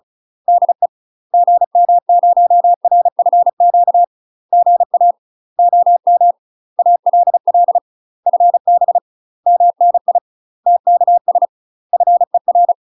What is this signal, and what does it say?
Morse code, heard as a keyed tone, about 35 wpm, 700 Hz. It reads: DE GM0RFQ GA OM ALL FB MNI TKS FER
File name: UK MASTER QSOs TRAINING / 35WPM / UK MASTER QSOs 0013